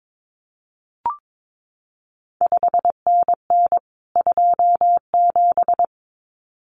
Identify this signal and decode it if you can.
Morse code, heard as a keyed tone, about 22 wpm, 700 Hz. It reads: 5NN 27
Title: CQ WW DX Contest Exchange Encore - 22wpm